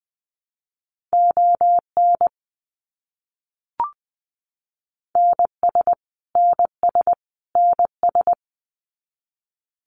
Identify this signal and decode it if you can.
Morse code, heard as a keyed tone, about 19 wpm, 700 Hz. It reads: ON NS NS NS
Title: Canadian Provinces and Territories - Repeated 3x - 20wpm